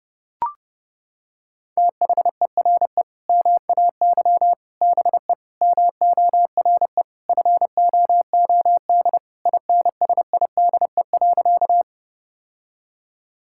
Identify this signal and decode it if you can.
Morse code, heard as a keyed tone, about 30 wpm, 700 Hz. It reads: THERE MAY BE MORE FOOD INSIDE.
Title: Sentences from Top 1000 Words - Review Part 6 - 30wpm